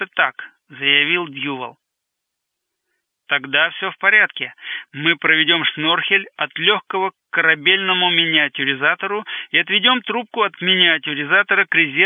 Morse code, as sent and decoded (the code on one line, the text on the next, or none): none